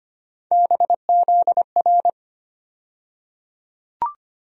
Morse code, heard as a keyed tone, about 25 wpm, 700 Hz. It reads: BZR
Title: USA Call Sign Postfixes - 3 Letters 25wpm